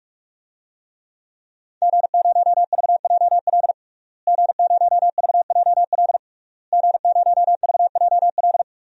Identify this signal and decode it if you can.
Morse code, heard as a keyed tone, about 46 wpm, 700 Hz. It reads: G0VJL G0VJL G0VJL
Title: DXCC-100 Call Signs - 5 Characters - Repeated 3x - 45wpm